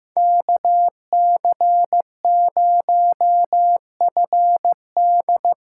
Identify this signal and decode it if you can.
Morse code, heard as a keyed tone, about 15 wpm, 700 Hz. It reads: KC0FD
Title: USA Call Signs - 6 Characters - 6x Character Spacing - 15wpm